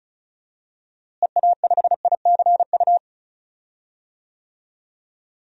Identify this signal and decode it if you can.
Morse code, heard as a keyed tone, about 35 wpm, 700 Hz. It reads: EA5ICU
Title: DXCC-100 Call Signs - 6 Characters - Repeated 3x - 35wpm